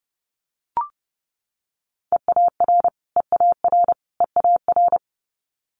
Morse code, heard as a keyed tone, about 30 wpm, 700 Hz. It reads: EAR EAR EAR
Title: Multi-letter Phonemes Repeated 3x 30wpm